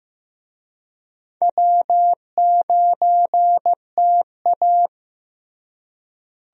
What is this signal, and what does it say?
Morse code, heard as a keyed tone, about 16 wpm, 700 Hz. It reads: W9TA
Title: USA Call Signs - 4 Characters - Repeated 3x - 15wpm